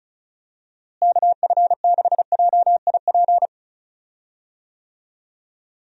Morse code, heard as a keyed tone, about 36 wpm, 700 Hz. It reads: KF6JIP
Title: USA Call Signs - 6 Characters - Repeated 3x - 35wpm